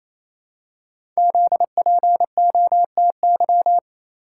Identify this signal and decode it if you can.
Morse code, heard as a keyed tone, about 29 wpm, 700 Hz. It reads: ZPOTY
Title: Code Groups - 5 Letters - 28wpm